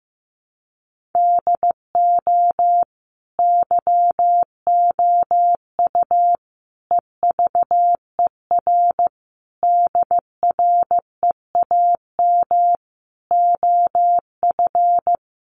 Morse code, heard as a keyed tone, about 15 wpm, 700 Hz. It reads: DO YOU EVER DREAM OF
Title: Sentences from Top 1500 Words 15wpm